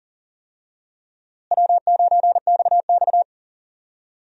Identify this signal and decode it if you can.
Morse code, heard as a keyed tone, about 40 wpm, 700 Hz. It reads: W9XX
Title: USA Call Signs - 4 Characters - Repeated 3x - 40wpm